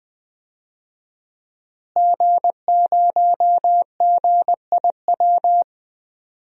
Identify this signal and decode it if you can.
Morse code, heard as a keyed tone, about 20 wpm, 700 Hz. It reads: G0GIW
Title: DXCC-100 Call Signs - 5 Characters - Repeated 3x - 20wpm